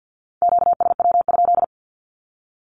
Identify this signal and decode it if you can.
Morse code, heard as a keyed tone, about 50 wpm, 700 Hz. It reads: QSW?
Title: Q-Codes with Spoken Meaning - 50wpm